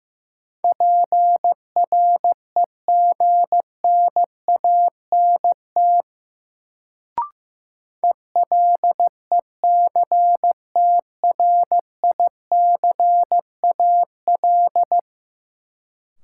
Morse code, heard as a keyed tone, about 15 wpm, 700 Hz. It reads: PREGNANT ELECTRICAL
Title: Sentences from Top 1000 Words - Review Part 6 - New Word Drill - 15wpm